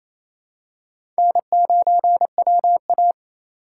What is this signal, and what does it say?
Morse code, heard as a keyed tone, about 28 wpm, 700 Hz. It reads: N9WA
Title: USA Call Signs - 4 Characters - 2x Character Spacing - 28wpm